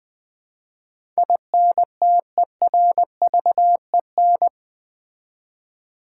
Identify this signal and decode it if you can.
Morse code, heard as a keyed tone, about 20 wpm, 700 Hz. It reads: INTERVEN
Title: Build A Word - 4 Letters At A Time - 12-Letter Words - 20wpm